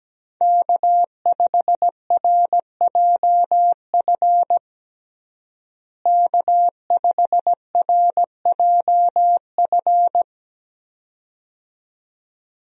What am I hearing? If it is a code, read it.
Morse code, heard as a keyed tone, about 17 wpm, 700 Hz. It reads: K5RJF K5RJF
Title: USA Call Signs - 5 Characters - Repeated 2x - 17wpm